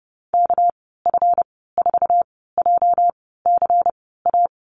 Morse code, heard as a keyed tone, about 30 wpm, spaced as for 17 wpm, 700 Hz. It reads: KF4JCA
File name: Lesson 37g - Letter x - Call Signs 30wpm